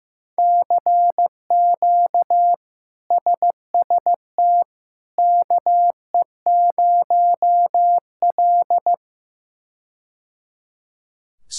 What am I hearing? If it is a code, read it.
Morse code, heard as a keyed tone, about 15 wpm, 700 Hz. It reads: CQ SST KE0L
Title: K1USN SST QSO Practice - 15wpm